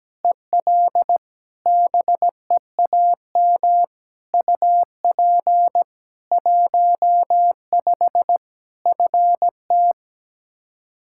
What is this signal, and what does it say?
Morse code, heard as a keyed tone, about 17 wpm, 700 Hz. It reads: EL BEAM UP 15 FT
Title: Practice QSO - Exchange 2 - Info - 17wpm